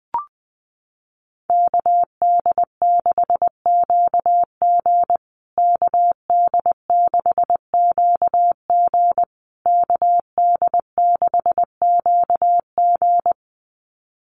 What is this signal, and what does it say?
Morse code, heard as a keyed tone, about 20 wpm, 700 Hz. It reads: KD6QG KD6QG KD6QG
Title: USA Call Signs Repeated 3x 20wpm